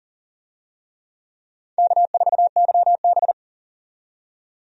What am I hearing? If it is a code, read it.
Morse code, heard as a keyed tone, about 40 wpm, 700 Hz. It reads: K4YB
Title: USA Call Signs - 4 Characters - 40wpm